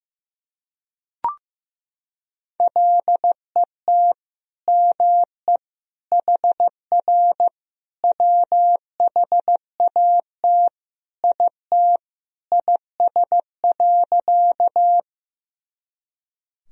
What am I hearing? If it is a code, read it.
Morse code, heard as a keyed tone, about 14 wpm, 700 Hz. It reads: LET ME HR WHAT IT IS.
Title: Sentences from Top 300 Words and Abbreviations 15wpm